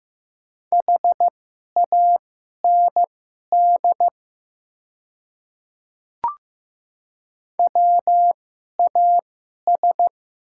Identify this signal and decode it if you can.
Morse code, heard as a keyed tone, about 15 wpm, spaced as for 11 wpm, 700 Hz. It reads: HAND WAS
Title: 2x Character Spacing - Top 500 Words - 15wpm